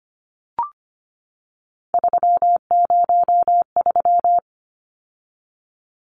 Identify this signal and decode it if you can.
Morse code, heard as a keyed tone, about 25 wpm, 700 Hz. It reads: 303
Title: Numbers - 3-Digit 25wpm